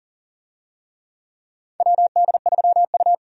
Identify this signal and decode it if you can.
Morse code, heard as a keyed tone, about 40 wpm, 700 Hz. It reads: WD3U
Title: USA Call Signs - 4 Characters - 40wpm